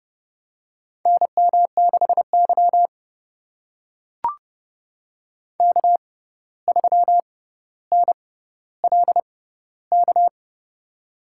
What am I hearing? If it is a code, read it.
Morse code, heard as a keyed tone, about 30 wpm, spaced as for 14 wpm, 700 Hz. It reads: NM6Y K3NLK
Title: USA Call Signs - 6x Character Spacing - 30wpm